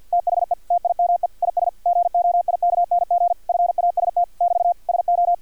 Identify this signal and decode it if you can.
Morse code, heard as a keyed tone, about 50 wpm, 700 Hz, with radio noise. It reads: THE TIME IS GOIKNG FAST <BT> SO
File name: archived / CRSnet